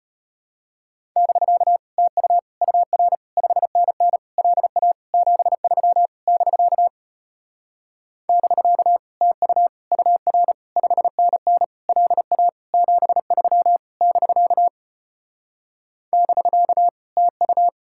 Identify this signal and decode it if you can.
Morse code, heard as a keyed tone, about 33 wpm, 700 Hz. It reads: <BK> TU UR 5NN LA 73 <BK> <BK> TU UR 5NN LA 73 <BK> <BK> TU